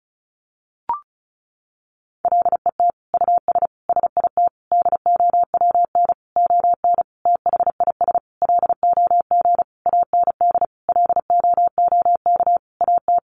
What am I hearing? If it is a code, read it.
Morse code, heard as a keyed tone, about 34 wpm, 700 Hz. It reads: LET US SIT DOWN ON THIS LOG AND LOOK AT
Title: Sentences from Top 1400 Words 35wpm